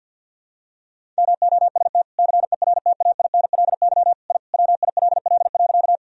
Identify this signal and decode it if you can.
Morse code, heard as a keyed tone, about 50 wpm, 700 Hz. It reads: MOST CERTAINLY I WILL.
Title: Sentences from Top 1000 Words - Review Part 1 - 50wpm